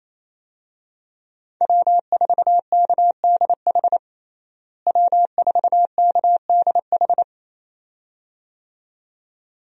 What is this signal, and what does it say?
Morse code, heard as a keyed tone, about 28 wpm, 700 Hz. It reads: W4KDH W4KDH
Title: USA Call Signs - 5 Characters - Repeated 2x - 28wpm